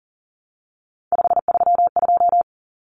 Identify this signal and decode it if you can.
Morse code, heard as a keyed tone, about 40 wpm, 700 Hz. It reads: 532
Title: Numbers - 3-Digit 40wpm